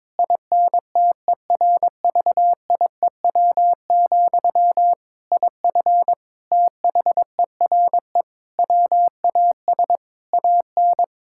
Morse code, heard as a keyed tone, about 22 wpm, 700 Hz. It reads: INTERVIEW, IF THERE WAS AN
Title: Sentences from Top 1600 Words 22wpm